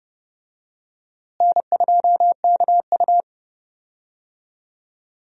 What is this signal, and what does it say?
Morse code, heard as a keyed tone, about 30 wpm, 700 Hz. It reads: N2KU
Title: USA Call Signs - 4 Characters - Repeated 2x - 30wpm